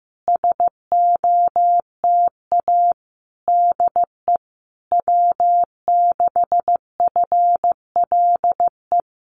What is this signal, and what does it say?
Morse code, heard as a keyed tone, about 14 wpm, 700 Hz. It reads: SOTA DE W6FLE
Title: Common QSO Phrases Encore 15wpm